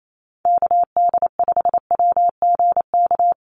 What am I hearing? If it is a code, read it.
Morse code, heard as a keyed tone, about 28 wpm, 700 Hz. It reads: KD5WGK
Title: USA Call Signs 28wpm